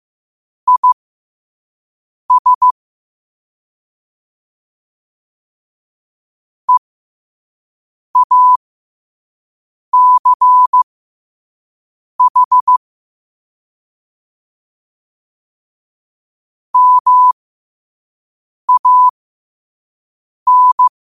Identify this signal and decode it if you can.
Morse code, heard as a keyed tone, about 15 wpm, spaced as for 5 wpm, 1000 Hz. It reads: IS EACH MAN
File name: qotdfiles / 5wpm